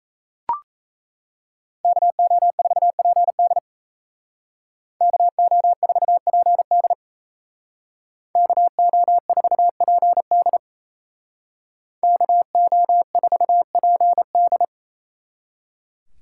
Morse code, heard as a keyed tone, about 35 wpm, 700 Hz. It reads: KO4PD KO4PD KO4PD KO4PD
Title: Speed Racing - USA 2x2 Call Signs 28wpm